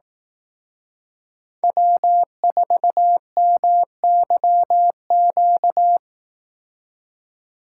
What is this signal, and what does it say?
Morse code, heard as a keyed tone, about 18 wpm, 700 Hz. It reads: W4MYQ
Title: Speed Racing - USA 1x3 Call Signs 15wpm